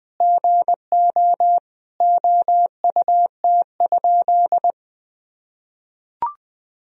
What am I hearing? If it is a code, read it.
Morse code, heard as a keyed tone, about 19 wpm, 700 Hz. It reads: GO OUT?
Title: Sentences from Top 1000 Words - Review Part 1 - 20wpm